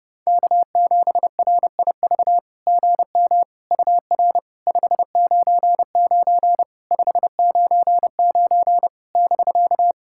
Morse code, heard as a keyed tone, about 30 wpm, 700 Hz. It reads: K7RIV GM UR 599 599 <BK>
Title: SOTA QSO Practice 30wpm